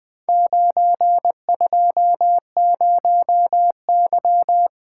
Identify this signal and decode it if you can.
Morse code, heard as a keyed tone, about 20 wpm, 700 Hz. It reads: 920Y
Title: Four Letter-Number Combinations - 20wpm